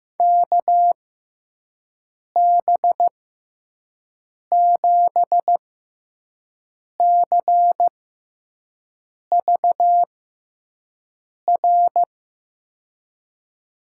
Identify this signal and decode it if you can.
Morse code, heard as a keyed tone, about 15 wpm, spaced as for 5 wpm, 700 Hz. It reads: KB7CVR
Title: USA Call Signs - 6 Characters - 6x Character Spacing - 15wpm